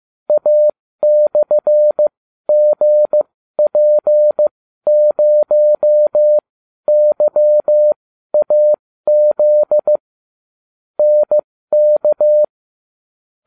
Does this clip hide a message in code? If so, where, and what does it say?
Yes: Morse code from 0.3 to 12.4 s, heard as a keyed tone, about 15 wpm, spaced as for 12 wpm, 600 Hz. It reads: A/GP0YAZ NK